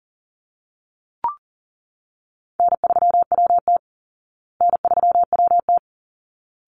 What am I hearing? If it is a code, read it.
Morse code, heard as a keyed tone, about 40 wpm, 700 Hz. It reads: N3WT N3WT
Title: Call Signs Repeated 3x - 40wpm